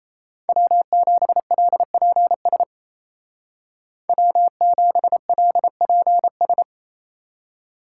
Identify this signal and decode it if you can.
Morse code, heard as a keyed tone, about 34 wpm, 700 Hz. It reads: W7LPS W7LPS
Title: Speed Racing - USA 1x3 Call Signs 28wpm